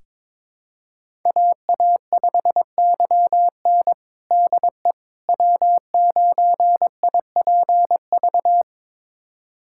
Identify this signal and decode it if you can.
Morse code, heard as a keyed tone, about 22 wpm, 700 Hz. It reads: AA5YN DE W9IPV
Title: USA Callsign DE Callsign - 5 Characters - 22wpm